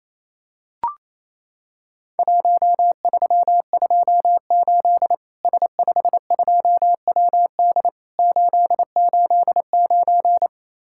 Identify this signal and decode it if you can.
Morse code, heard as a keyed tone, about 28 wpm, 700 Hz. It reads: 1328 S52WD 889
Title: Worked All Europe DX Contest - QTC Exchange - 28wpm